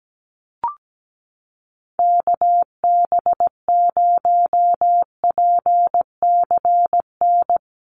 Morse code, heard as a keyed tone, about 17 wpm, 700 Hz. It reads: KB0PCN
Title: USA Call Signs Repeated 2x 17wpm